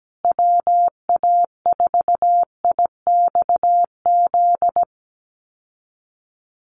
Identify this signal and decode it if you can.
Morse code, heard as a keyed tone, about 17 wpm, 700 Hz. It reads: WA4IXZ
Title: USA Call Signs Repeated 3x 17wpm